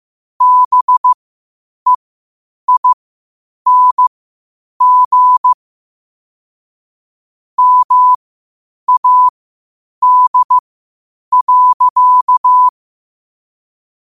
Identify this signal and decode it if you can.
Morse code, heard as a keyed tone, about 15 wpm, spaced as for 8 wpm, 1000 Hz. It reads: BEING MAD.
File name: qotdfiles / 8wpm